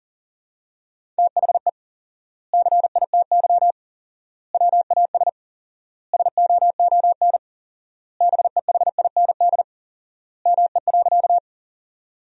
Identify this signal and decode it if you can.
Morse code, heard as a keyed tone, about 37 wpm, 700 Hz. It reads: THE CITY WAS SOON BEHIND ME.
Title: 4x Word-Spacing - Sentences from Top 500 Words - 40wpm